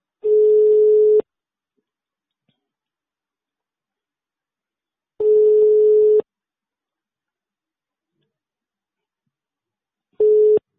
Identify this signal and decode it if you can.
Ringback tone (Europe, 425 Hz).